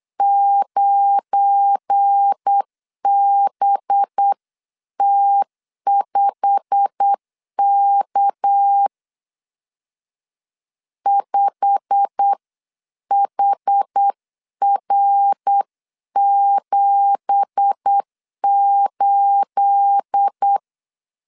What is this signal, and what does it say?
Morse code, heard as a keyed tone, about 8 wpm, 800 Hz. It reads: 9BT5K 5HR78
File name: 8wpm